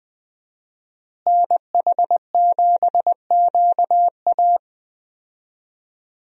Morse code, heard as a keyed tone, about 20 wpm, 700 Hz. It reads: NH7QA
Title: USA Call Signs - 5 Characters - Repeated 2x - 20wpm